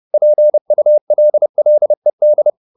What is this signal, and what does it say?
Morse code, heard as a keyed tone, about 30 wpm, 600 Hz. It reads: PULLED
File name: the old man and the sea / the old man and the sea 1 30wpm